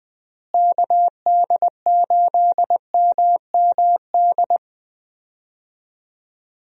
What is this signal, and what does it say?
Morse code, heard as a keyed tone, about 20 wpm, 700 Hz. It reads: KD8MMD
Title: USA Call Signs - 6 Characters - Repeated 3x - 20wpm